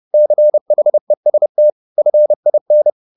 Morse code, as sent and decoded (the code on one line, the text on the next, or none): -.-. .... ... - / ..-. .. -.
CHEST FIN